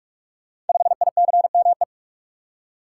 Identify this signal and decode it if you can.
Morse code, heard as a keyed tone, about 44 wpm, 700 Hz. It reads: 5ICME
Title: Code Groups - 5 Letter:Numbers - 45wpm